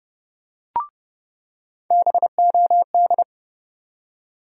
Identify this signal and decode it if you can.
Morse code, heard as a keyed tone, about 32 wpm, 700 Hz. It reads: BOD
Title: USA Call Sign Postfixes - 3 Letters 30wpm